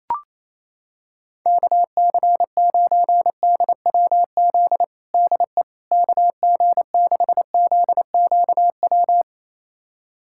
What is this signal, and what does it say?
Morse code, heard as a keyed tone, about 28 wpm, 700 Hz. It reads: KC9DWZ DE KG6ZQW
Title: USA Callsign DE Callsign - 6 Characters - 28wpm